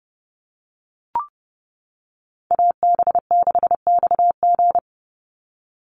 Morse code, heard as a keyed tone, about 30 wpm, 700 Hz. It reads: AB6XG
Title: Call Signs - 30wpm